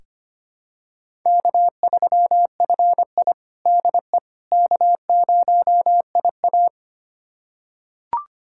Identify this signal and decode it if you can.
Morse code, heard as a keyed tone, about 25 wpm, 700 Hz. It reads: K3FI DE K0IA
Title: USA Callsign DE Callsign - 4 Characters - 25wpm